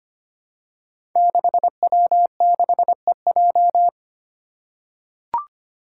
Morse code, heard as a keyed tone, about 25 wpm, 700 Hz. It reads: 6W6EJ
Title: Five Challenging Letter-Number Combinations 25wpm